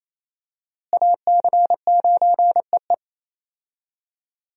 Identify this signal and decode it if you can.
Morse code, heard as a keyed tone, about 28 wpm, 700 Hz. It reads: AC9EE
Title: USA Call Signs - 5 Characters - Repeated 3x - 28wpm